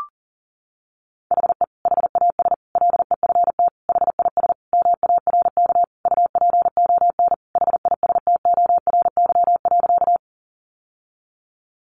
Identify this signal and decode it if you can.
Morse code, heard as a keyed tone, about 40 wpm, 700 Hz. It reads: HE HAS LEFT HIS MARK UPON HISTORY.
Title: Sentences from Top 1100 Words 40wpm